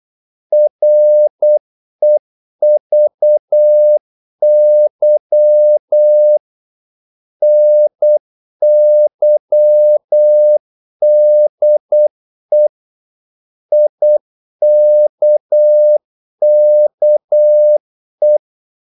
Morse code, heard as a keyed tone, about 8 wpm, 600 Hz. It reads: REVY NYDE IKKE
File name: Korte ord DK 8wpm